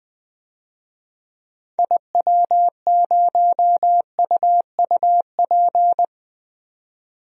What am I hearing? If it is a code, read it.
Morse code, heard as a keyed tone, about 21 wpm, 700 Hz. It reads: IW0UUP
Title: DXCC-100 Call Signs - 6 Characters - 20wpm